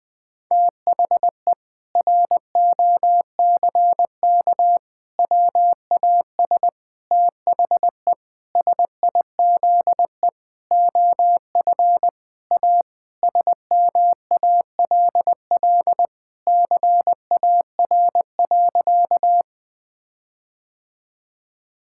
Morse code, heard as a keyed tone, about 20 wpm, 700 Hz. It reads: THE ROCK WAS THE SIZE OF A SMALL CAR.
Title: Sentences with 4-Word Phrases - Spoken Beginning - 20wpm